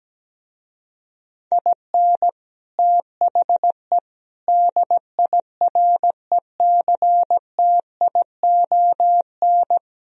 Morse code, heard as a keyed tone, about 17 wpm, 700 Hz. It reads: IN THE DIRECTION
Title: Speed-Racing - Sets of 5 Words 17wpm